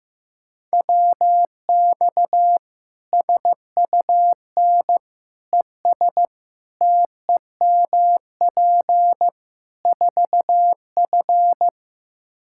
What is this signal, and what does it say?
Morse code, heard as a keyed tone, about 15 wpm, 700 Hz. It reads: WX SUN ES TEMP 4F
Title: Practice QSO - Exchange 2 - Info - 15wpm